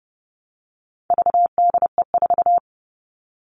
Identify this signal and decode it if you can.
Morse code, heard as a keyed tone, about 30 wpm, 700 Hz. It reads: VDE4
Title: All Cut Numbers - 4-Digit 30wpm